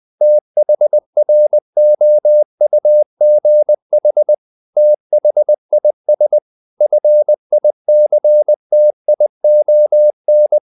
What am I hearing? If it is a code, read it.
Morse code, heard as a keyed tone, about 20 wpm, 600 Hz. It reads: THROUGH THIS FICTION
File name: the old man and the sea / the old man and the sea part 1 20wpm